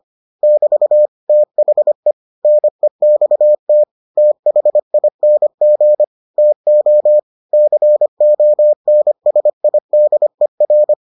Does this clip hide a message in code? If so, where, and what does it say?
Yes: Morse code from 0.4 to 10.9 s, heard as a keyed tone, about 25 wpm, 600 Hz. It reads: <BT> THE NEXT THING TO CONSIDER